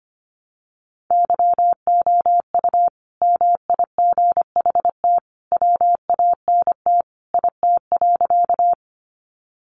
Morse code, heard as a keyed tone, about 25 wpm, 700 Hz. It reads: YOU MIGHT WANT IT.
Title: Sentences from Top 300 Words 25wpm